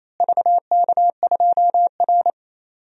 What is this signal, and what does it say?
Morse code, heard as a keyed tone, about 28 wpm, 700 Hz. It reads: VK2R
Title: DXCC-100 Call Signs - 4 Characters - Repeated 3x - 28wpm